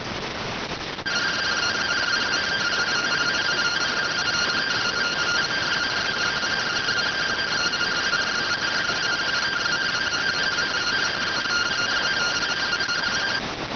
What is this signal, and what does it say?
Two-tone teleprinter signal, about 170 Hz shift.